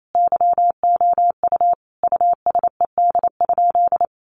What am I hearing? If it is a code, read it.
Morse code, heard as a keyed tone, about 28 wpm, 700 Hz. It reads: YOU USED?
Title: Sentences from Top 1200 Words 28wpm